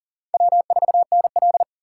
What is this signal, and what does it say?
Morse code, heard as a keyed tone, about 40 wpm, 700 Hz. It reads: W4NL
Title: USA Call Signs - 6x Character Spacing - 40wpm